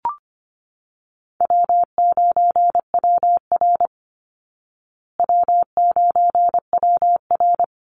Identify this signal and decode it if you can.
Morse code, heard as a keyed tone, about 25 wpm, 700 Hz. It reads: W9WR W9WR
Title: Easy Call Signs Repeated 3x - 25wpm